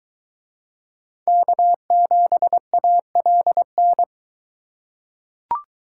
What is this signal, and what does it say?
Morse code, heard as a keyed tone, about 22 wpm, 700 Hz. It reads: K7ALN